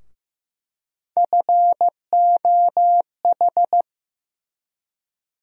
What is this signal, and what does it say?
Morse code, heard as a keyed tone, about 15 wpm, 700 Hz. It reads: FOH
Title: USA Call Sign Postfixes - 3 Letters 15wpm